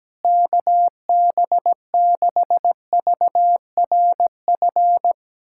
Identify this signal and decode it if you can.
Morse code, heard as a keyed tone, about 17 wpm, 700 Hz. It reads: KB6VRF
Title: USA Callsign DE Callsign - 6 Characters - 17wpm